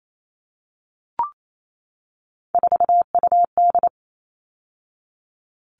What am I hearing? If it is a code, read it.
Morse code, heard as a keyed tone, about 28 wpm, 700 Hz. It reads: 4UD
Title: All Cut Numbers - 3-Digit 28wpm